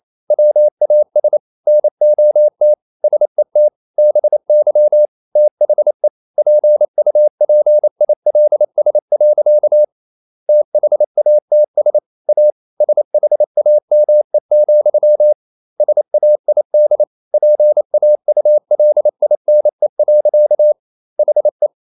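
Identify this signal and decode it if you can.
Morse code, heard as a keyed tone, about 28 wpm, 600 Hz. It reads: WAS NOT SET BY THE PUPILS. THATS A SHAME, SAID PAULINE. HE
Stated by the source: Morse code practice at 28 wpm, text of Ken Follett's novel Never in English, from CW med Gnister.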